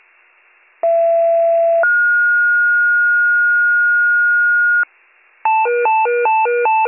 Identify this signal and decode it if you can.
Two-tone page, heard as a one-second tone, then a three-second tone, about 670 then 1430 Hz. It is heard from 0.8 to 4.8 s.